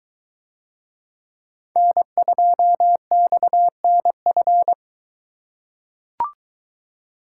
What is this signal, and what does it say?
Morse code, heard as a keyed tone, about 22 wpm, 700 Hz. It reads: N2XNF